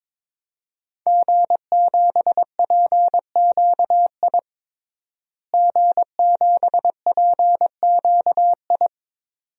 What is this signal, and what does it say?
Morse code, heard as a keyed tone, about 22 wpm, 700 Hz. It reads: G7PQI G7PQI
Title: DXCC-100 Call Signs - 5 Characters - Repeated 2x - 22wpm